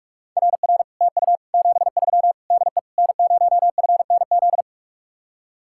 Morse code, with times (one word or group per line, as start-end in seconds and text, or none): 0.4-0.8 RR
1.0-1.4 TU
1.5-2.3 73
2.5-2.8 DE
3.0-4.6 N0FNZ